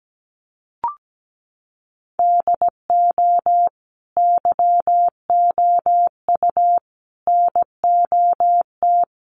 Morse code, heard as a keyed tone, about 17 wpm, 700 Hz. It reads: DO YOU NOT
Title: Sentences from Top 1100 Words 17wpm